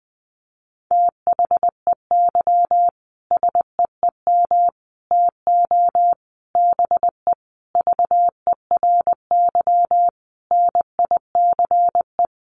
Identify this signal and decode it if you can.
Morse code, heard as a keyed tone, about 20 wpm, 700 Hz. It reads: THEY SEEM TO BE VERY NICE
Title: Sentences from Top 1100 Words 20wpm